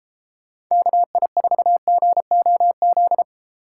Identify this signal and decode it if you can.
Morse code, heard as a keyed tone, about 36 wpm, 700 Hz. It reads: KI4GOZ